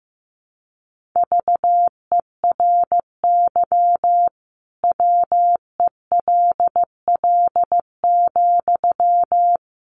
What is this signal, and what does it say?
Morse code, heard as a keyed tone, about 15 wpm, 700 Hz. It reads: VERY WELL,
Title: Sentences from Top 200 Words 15wpm